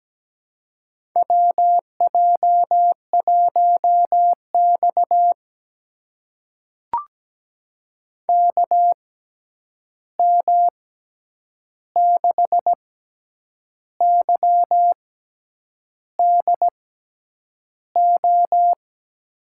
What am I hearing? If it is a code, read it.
Morse code, heard as a keyed tone, about 17 wpm, spaced as for 8 wpm, 700 Hz. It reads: WJ1X KM6YDO
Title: USA Call Signs - 6x Character Spacing - 17wpm